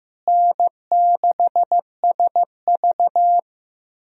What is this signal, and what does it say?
Morse code, heard as a keyed tone, about 15 wpm, 700 Hz. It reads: N6SV
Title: USA Call Signs - 4 Characters - Repeated 3x - 15wpm